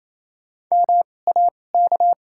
Morse code, heard as a keyed tone, about 28 wpm, spaced as for 20 wpm, 700 Hz. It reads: MAK
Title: 2x Character Spacing - Top 500 Words - 28wpm